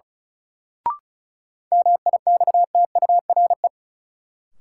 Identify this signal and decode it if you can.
Morse code, heard as a keyed tone, about 36 wpm, 700 Hz. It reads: MIXTURE
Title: Sentences from Top 1500 Words - New Word Drill Alt - 35wpm